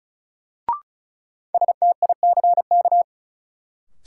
Morse code, heard as a keyed tone, about 34 wpm, 700 Hz. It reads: STICK
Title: Sentences from Top 1200 Words - New Word Drill - 35wpm